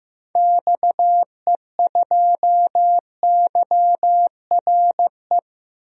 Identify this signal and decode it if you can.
Morse code, heard as a keyed tone, about 15 wpm, 700 Hz. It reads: XE2YRE
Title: DXCC-100 Call Signs - 6 Characters - Repeated 2x - 15wpm